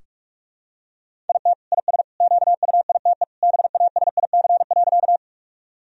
Morse code, heard as a keyed tone, about 46 wpm, 700 Hz. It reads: IT IS QUITE BASIC.
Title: Sentences from Top 1000 Words - Review Part 3 - 45wpm